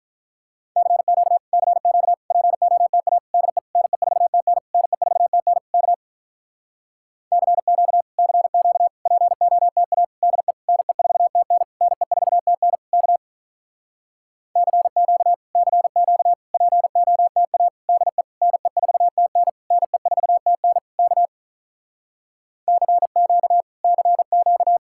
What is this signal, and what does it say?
Morse code, heard as a keyed tone, about 46 wpm, 700 Hz. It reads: CQ CQ POTA DE NE4TN NE4TN K CQ CQ POTA DE NE4TN NE4TN K CQ CQ POTA DE NE4TN NE4TN K CQ CQ